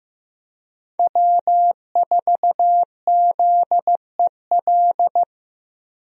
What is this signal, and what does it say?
Morse code, heard as a keyed tone, about 15 wpm, 700 Hz. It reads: W4ZEL
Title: USA Call Signs - 5 Characters - Repeated 2x - 15wpm